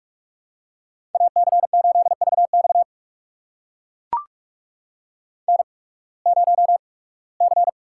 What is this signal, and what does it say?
Morse code, heard as a keyed tone, about 44 wpm, spaced as for 23 wpm, 700 Hz. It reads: AC8VX N0C
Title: USA Call Signs - 5 Characters - 8x Character Spacing - 45wpm